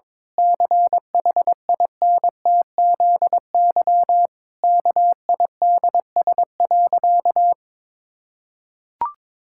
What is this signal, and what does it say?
Morse code, heard as a keyed tone, about 22 wpm, 700 Hz. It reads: CHINTZY KIDS.
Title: Pangram Challenge v2 - 22wpm